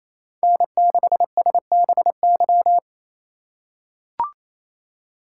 Morse code, heard as a keyed tone, about 28 wpm, 700 Hz. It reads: N6SBY
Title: USA Call Signs - 5 Characters - 4x Character Spacing - 28wpm